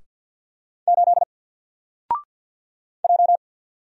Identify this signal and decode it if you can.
Morse code, heard as a keyed tone, about 48 wpm, 700 Hz. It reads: 8 J
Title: Single Letter-Number 50wpm